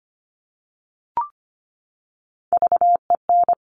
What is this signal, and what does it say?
Morse code, heard as a keyed tone, about 25 wpm, 700 Hz. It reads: VEN
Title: Top Three Letter Combinations in Words 25wpm